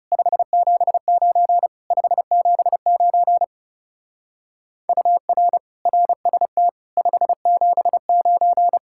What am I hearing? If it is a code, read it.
Morse code, heard as a keyed tone, about 36 wpm, 700 Hz. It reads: 579 579 UR RST 579
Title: Speed-Racing Common QSO Phrases - 30wpm